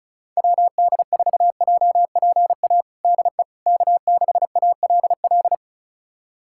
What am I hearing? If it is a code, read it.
Morse code, heard as a keyed tone, about 36 wpm, 700 Hz. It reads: WD4JPA DE K6ALL
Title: USA Callsign DE Callsign 35wpm